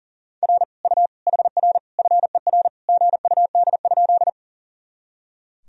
Morse code, heard as a keyed tone, about 40 wpm, 700 Hz. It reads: R U HR FER GUD?
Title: Sentences from Top 200 Words and Abbreviations 40wpm